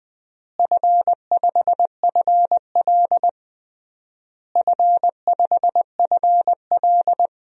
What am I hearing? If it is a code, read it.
Morse code, heard as a keyed tone, about 20 wpm, 700 Hz. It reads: F5FL F5FL
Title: DXCC-100 Call Signs - 4 Characters - Repeated 3x - 20wpm